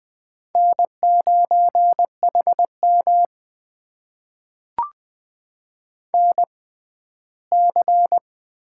Morse code, heard as a keyed tone, about 20 wpm, spaced as for 14 wpm, 700 Hz. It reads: N9HM NC
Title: USA Call Signs - 6x Character Spacing - 20wpm